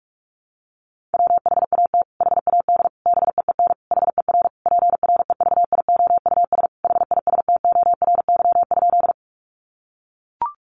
Morse code, heard as a keyed tone, about 44 wpm, 700 Hz. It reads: WHAT HAD BEEN HER PREVIOUS HISTORY?
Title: Sentences from Top 1500 Words 45wpm